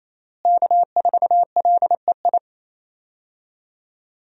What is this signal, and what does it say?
Morse code, heard as a keyed tone, about 28 wpm, 700 Hz. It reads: K4LEI
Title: USA Call Signs - 5 Characters - Repeated 3x - 28wpm